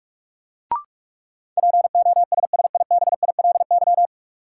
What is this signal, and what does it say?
Morse code, heard as a keyed tone, about 44 wpm, 700 Hz. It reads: POSSIBILY
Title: Sentences from Top 1300 Words - New Word Drill Alt - 45wpm